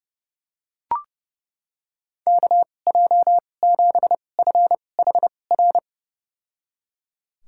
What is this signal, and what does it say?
Morse code, heard as a keyed tone, about 31 wpm, spaced as for 22 wpm, 700 Hz. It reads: KJ7FHR
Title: USA Call Signs - 6 Characters - 2x Character Spacing - 30wpm